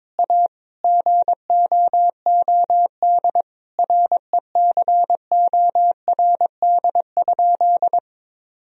Morse code, heard as a keyed tone, about 22 wpm, 700 Hz. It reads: A GOOD RECORD?
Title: Sentences from Top 1000 Words - Review Part 3 - 22wpm